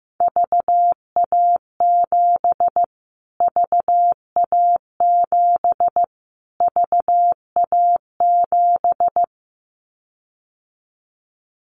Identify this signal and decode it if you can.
Morse code, heard as a keyed tone, about 15 wpm, 700 Hz. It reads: VA7 VA7 VA7
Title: Callsign Prefixes for Top 25 DXCC - Repeated 3x - 15wpm